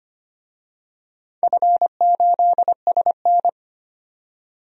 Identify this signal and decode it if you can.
Morse code, heard as a keyed tone, about 25 wpm, 700 Hz. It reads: F8SN
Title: DXCC-100 Call Signs - 4 Characters - 25wpm